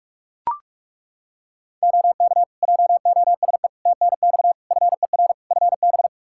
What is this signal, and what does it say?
Morse code, heard as a keyed tone, about 41 wpm, 700 Hz. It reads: OK JOSE TNX FER FB
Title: K1USN SST QSO Practice 45wpm